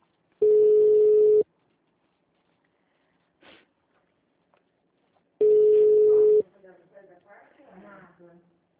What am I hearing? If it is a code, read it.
Ringback tone (Europe, 425 Hz), with speech.